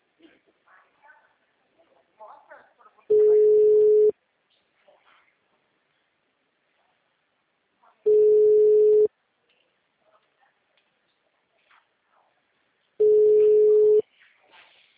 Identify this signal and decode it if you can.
Ringback tone (Europe, 425 Hz), with speech.